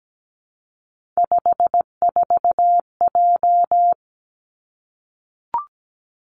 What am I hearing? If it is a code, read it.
Morse code, heard as a keyed tone, about 17 wpm, 700 Hz. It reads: 54J
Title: Three Challenging Letter-Number Combinations 17wpm